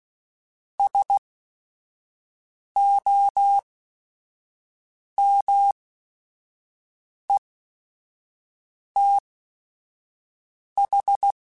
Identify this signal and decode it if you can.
Morse code, heard as a keyed tone, about 16 wpm, spaced as for 5 wpm, 800 Hz. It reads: SOMETH